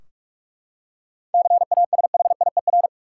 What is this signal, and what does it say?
Morse code, heard as a keyed tone, about 41 wpm, 700 Hz. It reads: CASHIER
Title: K1USN SST QSO Practice 45wpm